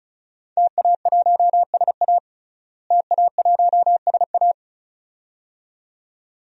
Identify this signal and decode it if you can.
Morse code, heard as a keyed tone, about 36 wpm, 700 Hz. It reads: TA1SA TA1SA
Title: DXCC-100 Call Signs - 5 Characters - Repeated 2x - 35wpm